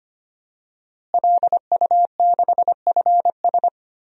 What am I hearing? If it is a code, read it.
Morse code, heard as a keyed tone, about 25 wpm, 700 Hz. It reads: LU6FS
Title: DXCC-100 Call Signs - 5 Characters - Repeated 2x - 25wpm